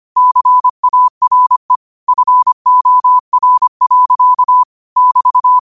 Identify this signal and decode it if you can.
Morse code, heard as a keyed tone, about 25 wpm, 1000 Hz. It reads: CARE FOR. <BT>
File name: qotdfiles / 25wpm